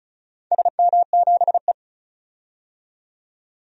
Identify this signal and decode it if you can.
Morse code, heard as a keyed tone, about 36 wpm, 700 Hz. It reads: SM7E
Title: DXCC-100 Call Signs - 4 Characters - 35wpm